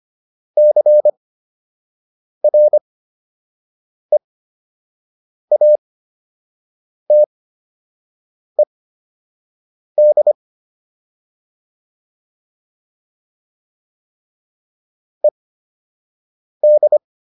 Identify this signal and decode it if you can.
Morse code, heard as a keyed tone, about 25 wpm, spaced as for 6 wpm, 600 Hz. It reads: CREATED ED